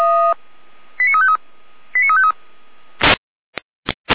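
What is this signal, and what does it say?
Five-tone selective call (ZVEI), heard as quick runs of five tones, tones of about 70 ms. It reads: 89353 89353
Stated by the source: Five-tone ZVEI selective call, received on SDR.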